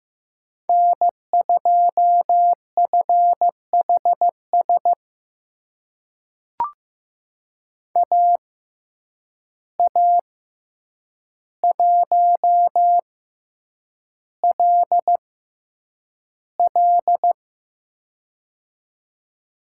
Morse code, heard as a keyed tone, about 15 wpm, spaced as for 8 wpm, 700 Hz. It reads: N2FHS AA1LL
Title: USA Call Signs - 5 Characters - 6x Character Spacing - 15wpm